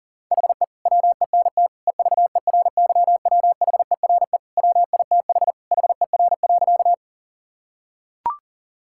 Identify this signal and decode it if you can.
Morse code, heard as a keyed tone, about 40 wpm, 700 Hz. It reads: HE WENT EVERYWHERE WITH HER.
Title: Sentences from Top 1600 Words 40wpm